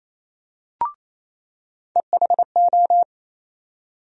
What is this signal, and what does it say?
Morse code, heard as a keyed tone, about 28 wpm, 700 Hz. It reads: EHO
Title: USA Call Sign Postfixes - 3 Letters 28wpm